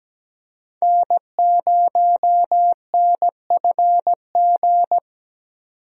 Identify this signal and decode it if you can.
Morse code, heard as a keyed tone, about 17 wpm, 700 Hz. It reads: N0NFG
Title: USA Call Signs - 5 Characters - 17wpm